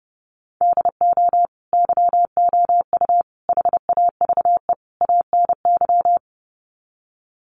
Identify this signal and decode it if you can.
Morse code, heard as a keyed tone, about 30 wpm, 700 Hz. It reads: DO YOU HAVE ANY
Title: Sets of 4 Words 30wpm